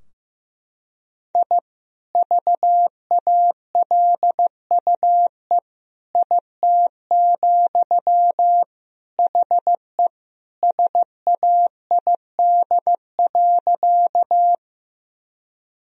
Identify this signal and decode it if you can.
Morse code, heard as a keyed tone, about 15 wpm, 700 Hz. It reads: I VALUE IT, HE SAID.
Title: Sentences from Top 1000 Words - Review Part 3 - 15wpm